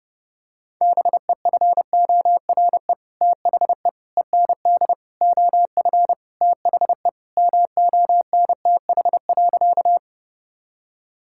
Morse code, heard as a keyed tone, about 30 wpm, 700 Hz. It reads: BEFORE THE END OF THE MONTH.
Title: Sentences from Top 1000 Words - Review Part 5 - 30wpm